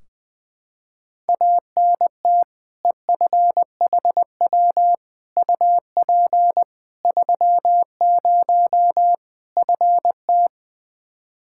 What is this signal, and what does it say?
Morse code, heard as a keyed tone, about 20 wpm, 700 Hz. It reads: ANT EFHW UP 30 FT
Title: Practice QSO - Exchange 2 - Info - 20wpm